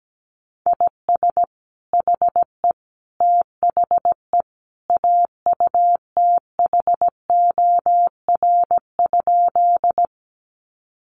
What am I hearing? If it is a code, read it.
Morse code, heard as a keyed tone, about 17 wpm, 700 Hz. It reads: IS HE THE AUTHOR?
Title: Sentences from Top 1200 Words 17wpm